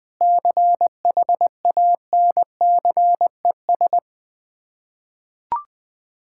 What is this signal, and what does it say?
Morse code, heard as a keyed tone, about 21 wpm, 700 Hz. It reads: CHANCES
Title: Sets of 8 Words - 20wpm